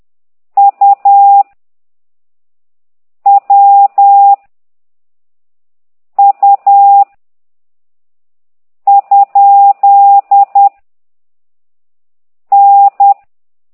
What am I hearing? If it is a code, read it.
Morse code, heard as a keyed tone, about 10 wpm, 800 Hz. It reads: U W U ? N
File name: ct2idw / exerc15